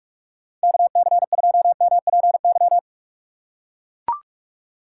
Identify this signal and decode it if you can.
Morse code, heard as a keyed tone, about 43 wpm, 700 Hz. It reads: KC2MPY
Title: USA Call Signs - 6 Characters - 4x Character Spacing - 45wpm